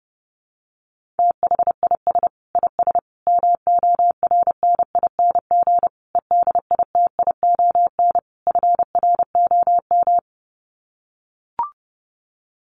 Morse code, heard as a keyed tone, about 30 wpm, 700 Hz. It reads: THIS IS MORNING EDITION FROM
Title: Sets of 5 Words Encore2 30wpm